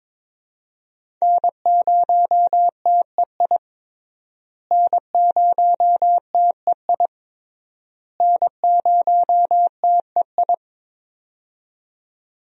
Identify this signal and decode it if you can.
Morse code, heard as a keyed tone, about 22 wpm, 700 Hz. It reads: N0TEI N0TEI N0TEI
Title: USA Call Signs - 5 Characters - Repeated 3x - 22wpm